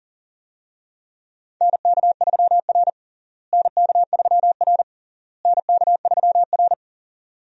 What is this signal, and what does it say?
Morse code, heard as a keyed tone, about 40 wpm, 700 Hz. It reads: NK3R NK3R NK3R
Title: USA Call Signs - 4 Characters - Repeated 3x - 40wpm